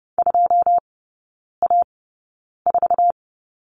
Morse code, heard as a keyed tone, about 30 wpm, spaced as for 9 wpm, 690 Hz. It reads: 2A4
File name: Lesson 17e - Letter u - Three Characters 30wpm